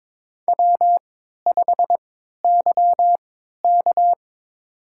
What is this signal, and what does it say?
Morse code, heard as a keyed tone, about 22 wpm, spaced as for 12 wpm, 700 Hz. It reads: W5YK
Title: USA Call Signs - 4 Characters - 3x Character Spacing - 22wpm